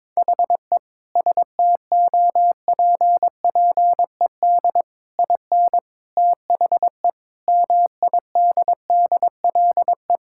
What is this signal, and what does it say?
Morse code, heard as a keyed tone, about 22 wpm, 700 Hz. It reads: HE STOPPED IN THE MIDDLE
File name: practice / 22wpm / Sentences with 3-Word Phrases - 22wpm